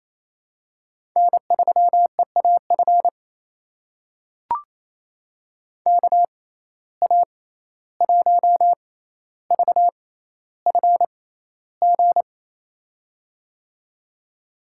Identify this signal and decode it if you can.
Morse code, heard as a keyed tone, about 28 wpm, spaced as for 14 wpm, 700 Hz. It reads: N3EAF KA1VFG
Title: USA Call Signs - 6x Character Spacing - 28wpm